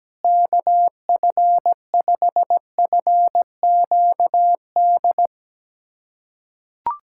Morse code, heard as a keyed tone, about 17 wpm, 700 Hz. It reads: KF5FQD
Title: USA Callsign DE Callsign - 6 Characters - 17wpm